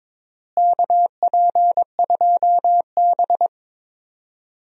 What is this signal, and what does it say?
Morse code, heard as a keyed tone, about 21 wpm, 700 Hz. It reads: KP2B
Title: DXCC-100 Call Signs - 4 Characters - Repeated 3x - 22wpm